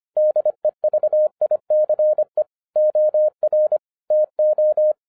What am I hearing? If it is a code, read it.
Morse code, heard as a keyed tone, about 25 wpm, 610 Hz. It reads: DEVICE OR TO
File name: 2011-10-mp3 / n09-CW TO PS2 CONVERTER 25wpm